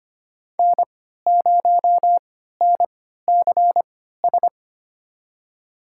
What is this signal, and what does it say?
Morse code, heard as a keyed tone, about 25 wpm, spaced as for 14 wpm, 700 Hz. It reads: N0NCS
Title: USA Call Signs - 5 Characters - 3x Character Spacing - 25wpm